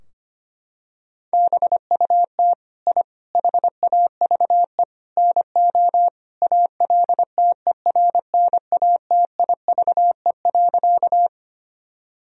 Morse code, heard as a keyed tone, about 25 wpm, 700 Hz. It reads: BUT I HAVE NO ALTERNATIVE.
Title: Sentences from Top 1000 Words - Review Part 4 - 25wpm